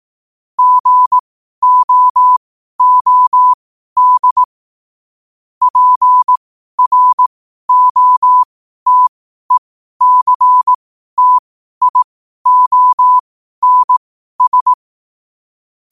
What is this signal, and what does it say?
Morse code, heard as a keyed tone, about 18 wpm, spaced as for 12 wpm, 1000 Hz. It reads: GOOD PROTECTIONS
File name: qotdfiles / 12wpm